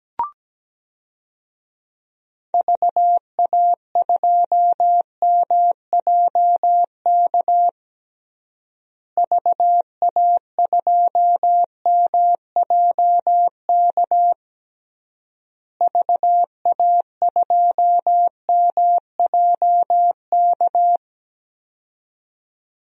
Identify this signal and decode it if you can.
Morse code, heard as a keyed tone, about 17 wpm, 700 Hz. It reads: VA2MJK VA2MJK VA2MJK
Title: DXCC-100 Call Signs - 6 Characters - Repeated 3x - 17wpm